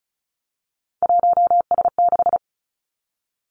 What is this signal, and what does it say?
Morse code, heard as a keyed tone, about 36 wpm, 700 Hz. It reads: 1S6
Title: Three Challenging Letter-Number Combinations 35wpm